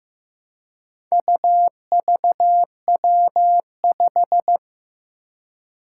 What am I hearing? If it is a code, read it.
Morse code, heard as a keyed tone, about 15 wpm, 700 Hz. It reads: UVW5
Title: Four Challenging Letter-Number Combinations 15wpm